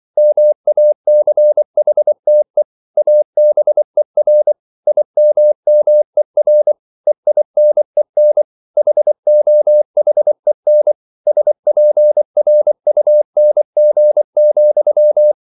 Morse code, heard as a keyed tone, about 24 wpm, 600 Hz. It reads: MACHTE ABER IMMER EINEN HOHEN SPRUNG,